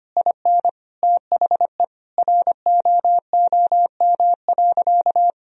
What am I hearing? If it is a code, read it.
Morse code, heard as a keyed tone, about 25 wpm, 700 Hz. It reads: IN THE ROOM.
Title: Sentences from Top 1000 Words - Review Part 4 - 25wpm